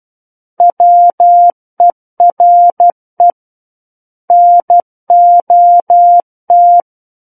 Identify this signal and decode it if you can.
Morse code, heard as a keyed tone, about 12 wpm, 700 Hz. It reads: WERE NOT
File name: Book One / 03-Chapter 3 - 12 WPM